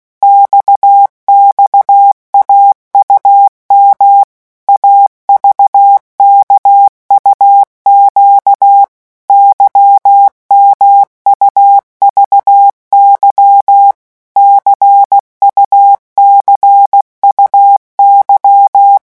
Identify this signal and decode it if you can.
Morse code, heard as a keyed tone, about 16 wpm, 800 Hz. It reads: XXAUM AVKUQ YMUVY CUCUY
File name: morse mp3 / revision2-vitesse 15 mots